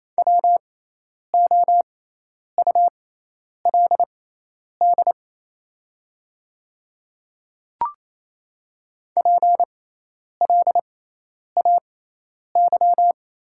Morse code, heard as a keyed tone, about 28 wpm, spaced as for 10 wpm, 700 Hz. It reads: WOULD PLAY
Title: 6x Character Spacing - Top 200 Words - 28wpm